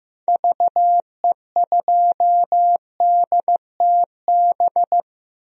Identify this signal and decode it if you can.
Morse code, heard as a keyed tone, about 15 wpm, 700 Hz. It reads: VE2DTB
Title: DXCC-100 Call Signs - 6 Characters - Repeated 3x - 15wpm